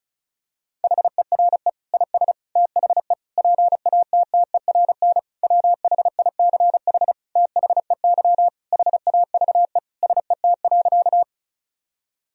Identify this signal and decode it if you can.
Morse code, heard as a keyed tone, about 34 wpm, 700 Hz. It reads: HERE IS THE PATTERN WHICH THEY HAVE SET.
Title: Sentences from Top 1000 Words - Review Part 2 - 35wpm